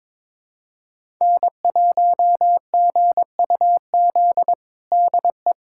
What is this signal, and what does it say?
Morse code, heard as a keyed tone, about 22 wpm, 700 Hz. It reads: N1GUZ DE
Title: USA Callsign DE Callsign 22wpm